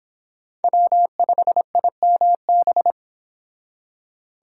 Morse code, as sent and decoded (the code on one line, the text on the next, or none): .-- ..... .. -- -...
W5IMB